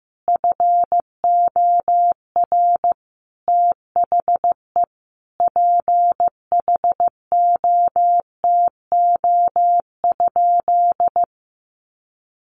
Morse code, heard as a keyed tone, about 15 wpm, 700 Hz. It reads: FOR THE PHOTO?
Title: Sentences from Top 1200 Words 15wpm